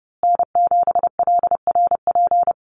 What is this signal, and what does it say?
Morse code, heard as a keyed tone, about 30 wpm, 700 Hz. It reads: N7LRP
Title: USA Call Signs Repeated 2x 30wpm